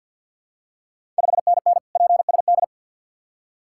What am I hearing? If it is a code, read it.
Morse code, heard as a keyed tone, about 52 wpm, 690 Hz. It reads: 5NN PSD
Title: Ontario QSO Party Exchange 50wpm